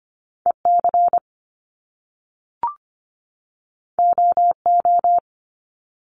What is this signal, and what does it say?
Morse code, heard as a keyed tone, about 25 wpm, 700 Hz. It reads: EC OO
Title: Top Two Letter Combinations in Words 25wpm